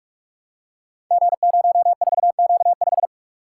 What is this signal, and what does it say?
Morse code, heard as a keyed tone, about 44 wpm, 700 Hz. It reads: G04Q5
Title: Code Groups - 5 Letter:Numbers - 45wpm